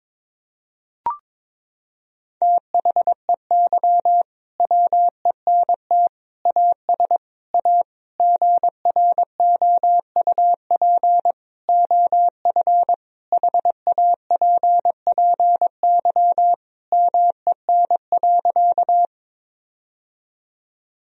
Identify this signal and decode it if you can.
Morse code, heard as a keyed tone, about 22 wpm, 700 Hz. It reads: THEY WENT AS A GROUP OF HAPPY MEN.
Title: Sentences from Top 1000 Words - Review Part 1 - 22wpm